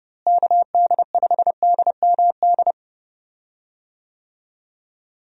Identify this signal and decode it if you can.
Morse code, heard as a keyed tone, about 32 wpm, 700 Hz. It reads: KD5DMD
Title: USA Call Signs - 6 Characters - Repeated 2x - 30wpm